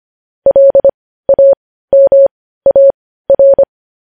Morse code, heard as a keyed tone, about 25 wpm, spaced as for 15 wpm, 550 Hz. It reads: LAMAR
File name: podcast / News headlines-1x-25x15